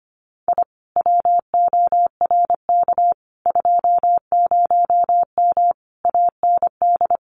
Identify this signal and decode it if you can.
Morse code, heard as a keyed tone, about 25 wpm, 700 Hz. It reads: I WORK 20M AND
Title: Common QSO Phrases Encore 25wpm